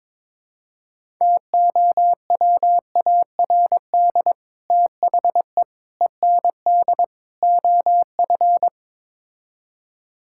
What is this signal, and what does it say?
Morse code, heard as a keyed tone, about 22 wpm, 700 Hz. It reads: TOWARD THE END OF
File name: practice / 22wpm / Build A Phrase - 5-Word Phrases - 22wpm